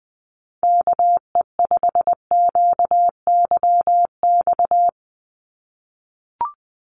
Morse code, heard as a keyed tone, about 20 wpm, 700 Hz. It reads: KE5QYX
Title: USA Call Signs 20wpm